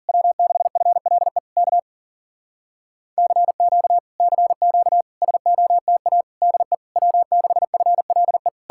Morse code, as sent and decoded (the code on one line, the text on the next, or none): .-- -.... ..-. .-.. / -.- / -.-. --.- / -.-. --.- / ... --- - .- / -.. / .-- -.... ..-. .-.. .
W6FLE K CQ CQ SOTA DE W6FLE